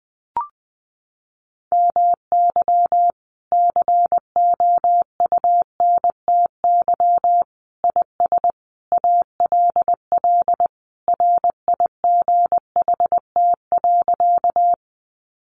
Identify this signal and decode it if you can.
Morse code, heard as a keyed tone, about 20 wpm, 700 Hz. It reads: MY COUNTY IS ALL RIGHT.
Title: Sentences from Top 1200 Words 20wpm